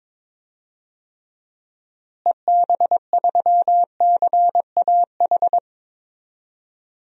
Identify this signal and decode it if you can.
Morse code, heard as a keyed tone, about 21 wpm, 700 Hz. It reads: EB3CAH
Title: DXCC-100 Call Signs - 6 Characters - Repeated 2x - 22wpm